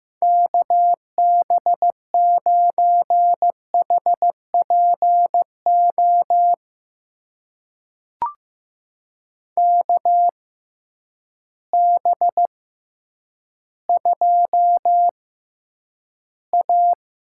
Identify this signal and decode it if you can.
Morse code, heard as a keyed tone, about 15 wpm, spaced as for 9 wpm, 700 Hz. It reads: KB9HPO KB2A
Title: USA Call Signs - 6 Characters - 6x Character Spacing - 15wpm